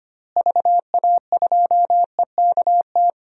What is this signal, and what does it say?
Morse code, heard as a keyed tone, about 25 wpm, 700 Hz. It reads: VA2EKT
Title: DXCC-100 Call Signs - 6 Characters - Repeated 2x - 25wpm